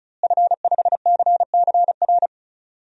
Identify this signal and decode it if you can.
Morse code, heard as a keyed tone, about 34 wpm, 700 Hz. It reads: F5CCR
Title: DXCC-100 Call Signs - 5 Characters - Repeated 3x - 35wpm